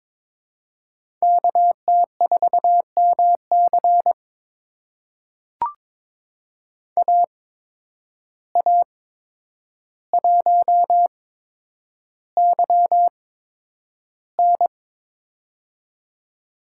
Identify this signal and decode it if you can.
Morse code, heard as a keyed tone, about 23 wpm, spaced as for 10 wpm, 700 Hz. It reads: KT4MC AA1YN
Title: USA Call Signs - 5 Characters - 8x Character Spacing - 22wpm